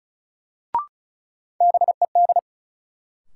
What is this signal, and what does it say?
Morse code, heard as a keyed tone, about 36 wpm, 700 Hz. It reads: BED
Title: Sentences from Top 500 Words - New Word Drill - 35wpm